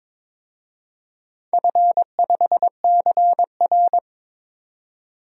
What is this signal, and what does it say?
Morse code, heard as a keyed tone, about 21 wpm, 700 Hz. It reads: F5CR
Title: DXCC-100 Call Signs - 4 Characters - 22wpm